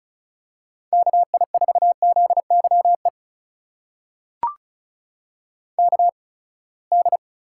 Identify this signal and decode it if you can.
Morse code, heard as a keyed tone, about 36 wpm, spaced as for 24 wpm, 700 Hz. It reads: KI4ZYE KD
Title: USA Call Signs - 8x Character Spacing - 35wpm